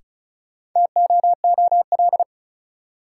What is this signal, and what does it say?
Morse code, heard as a keyed tone, about 36 wpm, 700 Hz. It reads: TOOL
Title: Sentences from Top 1000 Words - Review Part 6 - New Word Drill - 35wpm